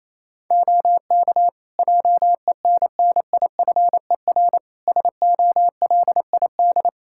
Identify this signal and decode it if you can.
Morse code, heard as a keyed tone, about 28 wpm, 700 Hz. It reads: OK JENNIFER SOLID
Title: Practice QSO - Exchange 3 - More Info - 28wpm